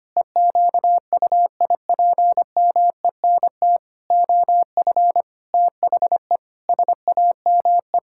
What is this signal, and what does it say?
Morse code, heard as a keyed tone, about 25 wpm, 700 Hz. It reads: EQUIPMENT OF THE SAME
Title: Pangram Challenge v2 - 25wpm